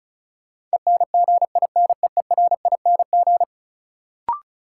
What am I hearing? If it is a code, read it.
Morse code, heard as a keyed tone, about 36 wpm, 700 Hz. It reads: ENGINEERING
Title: Sentences from Top 1500 Words - New Word Drill - 35wpm